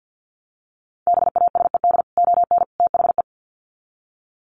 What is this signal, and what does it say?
Morse code, heard as a keyed tone, about 50 wpm, 700 Hz. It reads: BASED ON THE
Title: Sets of 3 Words 50wpm